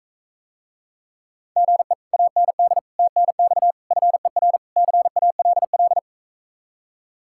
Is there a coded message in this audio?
Yes: Morse code from 1.6 to 6.0 s, heard as a keyed tone, about 45 wpm, 700 Hz. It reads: GE AND TNX FER CALL